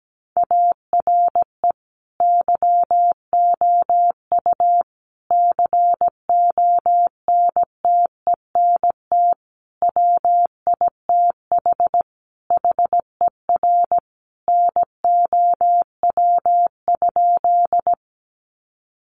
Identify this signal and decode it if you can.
Morse code, heard as a keyed tone, about 17 wpm, 700 Hz. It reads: ARE YOU CONTENT WITH HER NOW?
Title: Sentences from Top 900 Words 17wpm